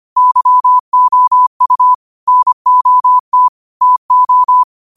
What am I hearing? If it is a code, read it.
Morse code, heard as a keyed tone, about 25 wpm, 1000 Hz. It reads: YOU NOT TO
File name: qotdfiles / 25wpm